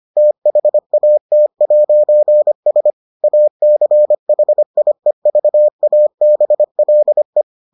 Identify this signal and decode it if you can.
Morse code, heard as a keyed tone, about 25 wpm, 600 Hz. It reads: THAT'S ACHIEVABLE